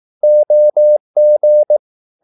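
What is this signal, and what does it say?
Morse code, heard as a keyed tone, about 18 wpm, 600 Hz. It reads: OG
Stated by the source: Slow (QRS) Morse practice at 18 wpm, CW med Gnister.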